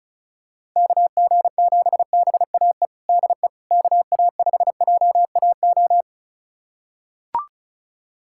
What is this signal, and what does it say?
Morse code, heard as a keyed tone, about 36 wpm, 700 Hz. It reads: KG7BAE DE KA5JAO
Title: USA Callsign DE Callsign - 6 Characters - 35wpm